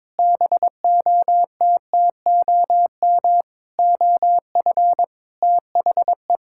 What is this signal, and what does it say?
Morse code, heard as a keyed tone, about 22 wpm, 700 Hz. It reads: BOTTOM OF THE
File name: practice / 22wpm / Sentences with 4-Word Phrases - 22wpm